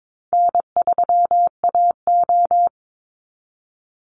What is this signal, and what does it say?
Morse code, heard as a keyed tone, about 22 wpm, 690 Hz. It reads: N3AO
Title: Easy Call Signs Repeated 2x - 22wpm